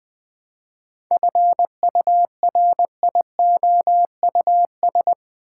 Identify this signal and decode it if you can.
Morse code, heard as a keyed tone, about 20 wpm, 700 Hz. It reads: FURIOUS
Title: 7 Letter Words - First 3 Letters Given - 20wpm